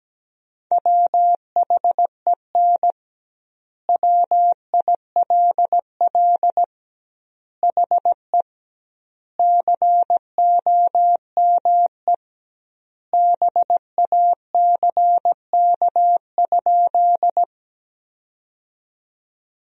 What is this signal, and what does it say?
Morse code, heard as a keyed tone, about 17 wpm, 700 Hz. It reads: WHEN WILL HE COME BACK?
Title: 2x Word-Spacing - Sentences from Top 200 Words - 17wpm